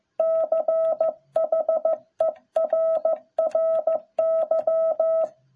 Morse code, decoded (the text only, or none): CHERRY